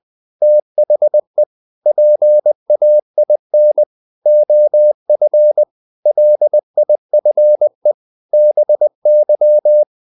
Morse code, heard as a keyed tone, about 20 wpm, 600 Hz. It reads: THE PAIN OF LIFE BY
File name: the old man and the sea part 2 20wpm